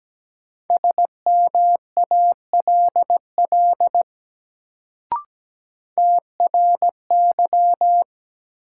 Morse code, heard as a keyed tone, about 17 wpm, 700 Hz. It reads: SMALL TRY
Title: Sentences from Top 200 Words - New Word Drill - 17wpm